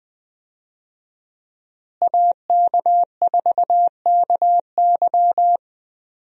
Morse code, heard as a keyed tone, about 20 wpm, 700 Hz. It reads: AK4KY
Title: USA Call Signs - 5 Characters - Repeated 3x - 20wpm